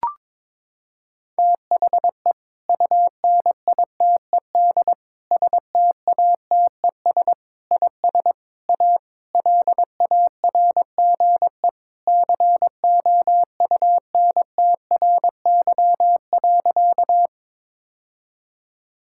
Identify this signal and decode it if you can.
Morse code, heard as a keyed tone, about 22 wpm, 700 Hz. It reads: THE UNITED STATES IS A LARGE COUNTRY.
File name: practice / 22wpm / Sentences with 4-Word Phrases - 22wpm